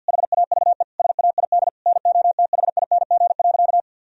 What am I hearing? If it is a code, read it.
Morse code, heard as a keyed tone, about 50 wpm, 700 Hz. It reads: HAVE SAID NOTHING.
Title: Sentences from Top 1000 Words - Review Part 4 - 50wpm